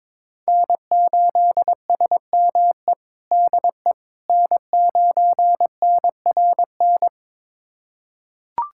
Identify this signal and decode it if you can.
Morse code, heard as a keyed tone, about 22 wpm, 700 Hz. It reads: N8SME DE N9NRN
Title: USA Callsign DE Callsign - 5 Characters - 22wpm